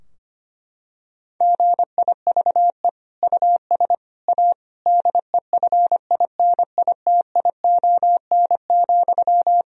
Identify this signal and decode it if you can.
Morse code, heard as a keyed tone, about 25 wpm, 700 Hz. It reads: GIVE US A DEFINITION,
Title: Sentences from Top 1000 Words - Review Part 4 - 25wpm